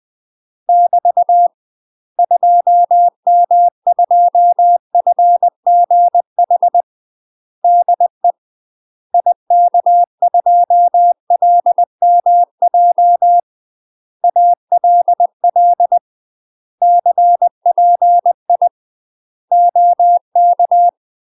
Morse code, heard as a keyed tone, about 20 wpm, 700 Hz. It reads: <BT> 2M2FGH DE IK2LMJ ALL CPI OK
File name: UK MASTER QSOs TRAINING / 20WPM / UK MASTER QSOs 0068